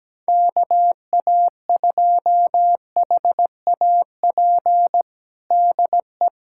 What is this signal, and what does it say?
Morse code, heard as a keyed tone, about 17 wpm, 700 Hz. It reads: KA2HAP DE
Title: USA Callsign DE Callsign - 6 Characters - 17wpm